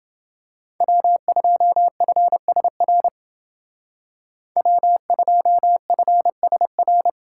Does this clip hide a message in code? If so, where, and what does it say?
Yes: Morse code from 0.8 to 7.1 s, heard as a keyed tone, about 29 wpm, 700 Hz. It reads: W2FSR W2FSR